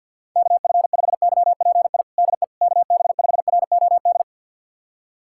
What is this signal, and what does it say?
Morse code, heard as a keyed tone, about 53 wpm, 700 Hz. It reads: KF5YPI DE KB5ROD
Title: USA Callsign DE Callsign 50wpm